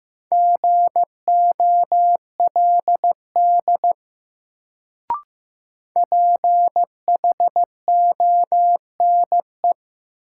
Morse code, heard as a keyed tone, about 15 wpm, 700 Hz. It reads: GOLD PHONE
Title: Sentences from Top 1100 Words - New Word Drill Alt - 15wpm